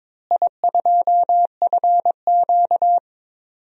Morse code, heard as a keyed tone, about 22 wpm, 700 Hz. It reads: I2FQ
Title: DXCC-100 Call Signs - 4 Characters - Repeated 3x - 22wpm